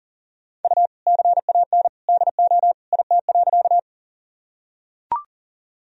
Morse code, heard as a keyed tone, about 40 wpm, 700 Hz. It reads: U CAN DO IT.
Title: Sentences from Top 100 Words and Abbreviations 40wpm